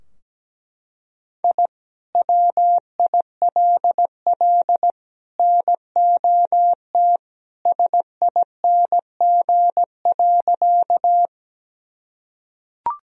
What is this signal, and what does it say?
Morse code, heard as a keyed tone, about 17 wpm, 700 Hz. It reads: I WILL NOT SING.
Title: Sentences from Top 1000 Words - Review Part 2 - 17wpm